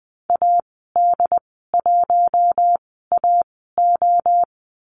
Morse code, heard as a keyed tone, about 20 wpm, spaced as for 15 wpm, 700 Hz. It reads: AD1AO
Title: 082 Lesson 14g - Number 2 - Call Signs 20wpm